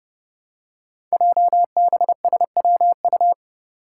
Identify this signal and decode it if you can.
Morse code, heard as a keyed tone, about 30 wpm, 700 Hz. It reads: JBSWU
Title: Code Groups - 5 Letter:Numbers - 30wpm